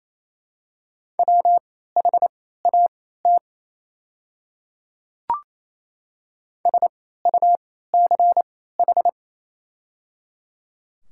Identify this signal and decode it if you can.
Morse code, heard as a keyed tone, about 28 wpm, spaced as for 16 wpm, 700 Hz. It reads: WHAT SUCH
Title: 3x Character Spacing - Top 200 Words - 28wpm